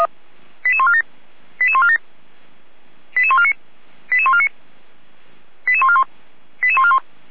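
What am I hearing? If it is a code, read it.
Five-tone selective call (ZVEI), heard as quick runs of five tones, tones of about 70 ms. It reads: 88147 88147 88149 88149 88141 88141
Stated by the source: ZVEI five-tone selective call 88140 received by SDR, fire brigade feuerwehr-orlamuende.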